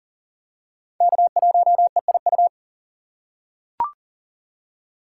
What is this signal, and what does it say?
Morse code, heard as a keyed tone, about 40 wpm, 700 Hz. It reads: K1EIU
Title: USA Call Signs - 5 Characters - 6x Character Spacing - 40wpm